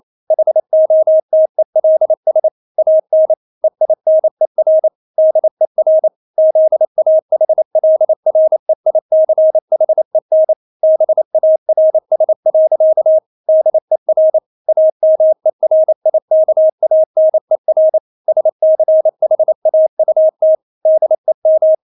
Morse code, heard as a keyed tone, about 28 wpm, 630 Hz. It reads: HOTELS AN EINER DER ZAHLREICHEN BARS. DER AMERIKANER SCHAUT DEM
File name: Witz9 28wpm